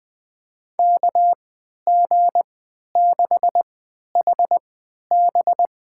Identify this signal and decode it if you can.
Morse code, heard as a keyed tone, about 20 wpm, spaced as for 11 wpm, 700 Hz. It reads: KG6HB
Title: USA Call Signs - 3x Character Spacing - 20wpm